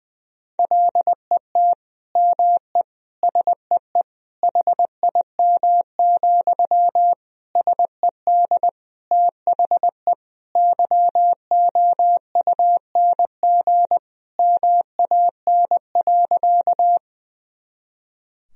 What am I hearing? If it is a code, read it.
Morse code, heard as a keyed tone, about 21 wpm, 700 Hz. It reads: LET ME SEE HIM, SED THE YOUNG MAN.
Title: Sentences from Top 400 Words and Abbreviations 20wpm